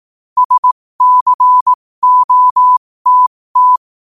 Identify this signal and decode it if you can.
Morse code, heard as a keyed tone, about 18 wpm, spaced as for 15 wpm, 1000 Hz. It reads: SCOTT
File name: qotdfiles / 15wpm